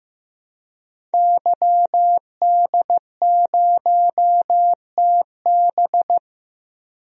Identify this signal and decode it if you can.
Morse code, heard as a keyed tone, about 15 wpm, 700 Hz. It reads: YD0TB
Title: DXCC-100 Call Signs - 5 Characters - Repeated 2x - 15wpm